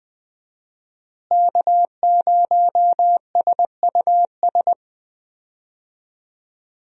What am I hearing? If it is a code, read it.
Morse code, heard as a keyed tone, about 20 wpm, 700 Hz. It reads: K0SUS
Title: USA Call Signs - 5 Characters - Repeated 2x - 20wpm